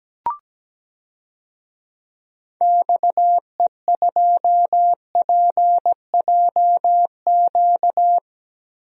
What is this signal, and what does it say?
Morse code, heard as a keyed tone, about 17 wpm, 700 Hz. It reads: XE2PJQ
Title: DXCC-100 Call Signs - 6 Characters - 17wpm